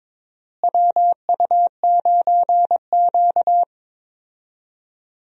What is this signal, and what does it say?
Morse code, heard as a keyed tone, about 22 wpm, 700 Hz. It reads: WU9Q
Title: USA Call Signs - 4 Characters - Repeated 2x - 22wpm